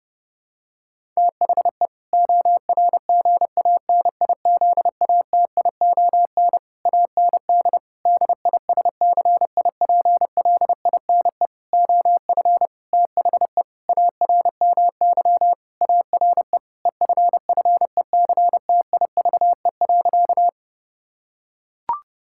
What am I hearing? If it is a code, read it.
Morse code, heard as a keyed tone, about 30 wpm, 700 Hz. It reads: THE ORGANIZATION AND DISCIPLINE OF THE ARMY ARE EFFECTIVE.
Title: Sentences from Top 1000 Words - Review Part 6 - 30wpm